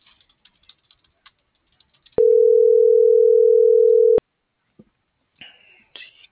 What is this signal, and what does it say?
Ringback tone (North America), 440 + 480 Hz.